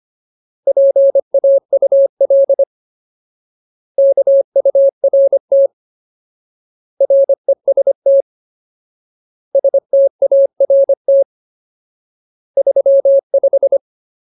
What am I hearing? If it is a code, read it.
Morse code, heard as a keyed tone, about 25 wpm, 550 Hz. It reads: PAUL KURT REST START 35
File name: Ultimate Word List / 25WPM Ultimate